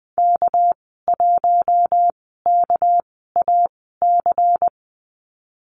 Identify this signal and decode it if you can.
Morse code, heard as a keyed tone, about 20 wpm, spaced as for 15 wpm, 700 Hz. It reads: K1KAC
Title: 219 Lesson 35g - Number 0 - Call Signs 20wpm